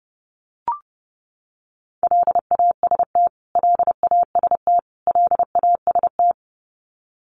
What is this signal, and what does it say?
Morse code, heard as a keyed tone, about 30 wpm, 700 Hz. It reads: LAST LAST LAST
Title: Top 500 Words Repeated 30wpm